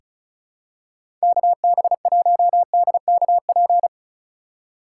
Morse code, heard as a keyed tone, about 36 wpm, 700 Hz. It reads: KB1DKP
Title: USA Call Signs - 6 Characters - 3x Character Spacing - 35wpm